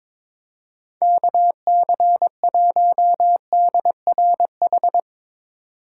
Morse code, heard as a keyed tone, about 22 wpm, 700 Hz. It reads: KC1DRH
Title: USA Call Signs - 6 Characters - 2x Character Spacing - 22wpm